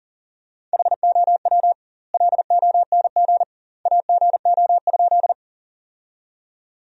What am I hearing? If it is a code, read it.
Morse code, heard as a keyed tone, about 40 wpm, 700 Hz. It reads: HOW LONG AGO?
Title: 2x Word-Spacing - Sentences from Top 500 Words - 40wpm